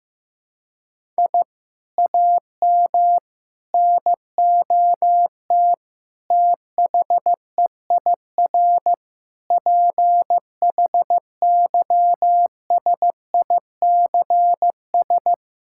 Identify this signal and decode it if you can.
Morse code, heard as a keyed tone, about 15 wpm, 700 Hz. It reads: I AM NOT THEIR PHYSICS
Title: Sentences from Top 1000 Words - Review Part 6 - 15wpm